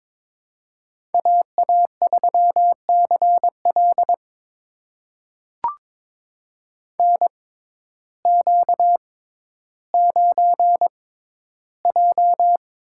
Morse code, heard as a keyed tone, about 22 wpm, spaced as for 12 wpm, 700 Hz. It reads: AA3CL NQ9J
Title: USA Call Signs - 6x Character Spacing - 22wpm